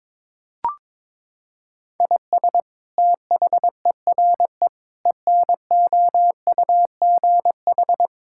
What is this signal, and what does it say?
Morse code, heard as a keyed tone, about 22 wpm, 700 Hz. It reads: IS THERE ENOUGH
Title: Sentences from Top 1000 Words - Review Part 5 - 22wpm